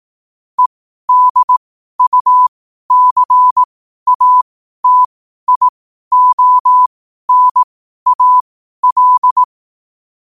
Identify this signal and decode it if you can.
Morse code, heard as a keyed tone, about 18 wpm, spaced as for 12 wpm, 1000 Hz. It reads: EDUCATIONAL
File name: qotdfiles / 12wpm